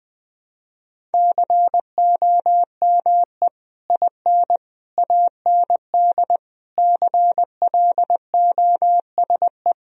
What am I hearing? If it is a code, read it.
Morse code, heard as a keyed tone, about 20 wpm, 700 Hz. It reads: COME IN AND CLOSE
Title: Sentences from Top 1000 Words - Review Part 1 - 20wpm